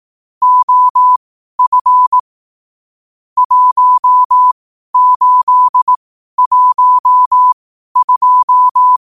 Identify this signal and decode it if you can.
Morse code, heard as a keyed tone, about 18 wpm, spaced as for 12 wpm, 1000 Hz. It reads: OF 1812